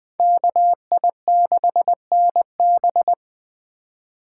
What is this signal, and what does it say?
Morse code, heard as a keyed tone, about 20 wpm, 700 Hz. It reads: KI6NB
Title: USA Call Signs - 5 Characters - 4x Character Spacing - 20wpm